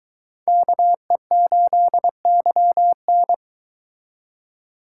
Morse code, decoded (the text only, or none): KE8YN